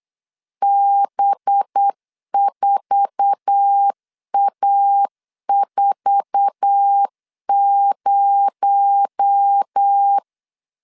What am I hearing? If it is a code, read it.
Morse code, heard as a keyed tone, about 8 wpm, 800 Hz. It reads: B4A40
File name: LEZIONI / 8 YJ45 / 8 YJ4 8wpm 3246